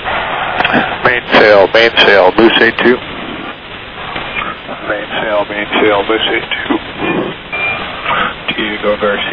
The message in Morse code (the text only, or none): none